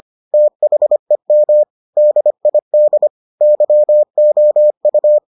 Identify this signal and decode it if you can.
Morse code, heard as a keyed tone, about 25 wpm, 600 Hz. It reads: THEM DID YOU